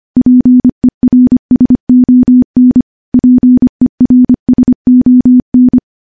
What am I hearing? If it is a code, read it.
Morse code, heard as a keyed tone, about 25 wpm, 250 Hz. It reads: PERSON PERSON
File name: morse / morse-training-25WPM